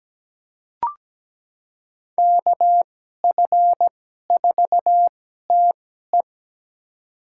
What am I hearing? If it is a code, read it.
Morse code, heard as a keyed tone, about 17 wpm, spaced as for 12 wpm, 700 Hz. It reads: KF4TE
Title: USA Call Signs - 5 Characters - 2x Character Spacing - 17wpm